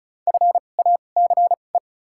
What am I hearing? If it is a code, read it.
Morse code, heard as a keyed tone, about 36 wpm, spaced as for 25 wpm, 700 Hz. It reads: FACE
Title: 2x Character Spacing - Top 500 Words - 35wpm